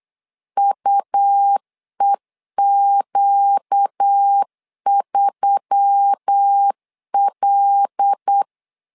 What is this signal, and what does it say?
Morse code, heard as a keyed tone, about 8 wpm, 800 Hz. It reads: UEQ3L
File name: LEZIONI / 5 LBVC / 5 L 8wpm 2831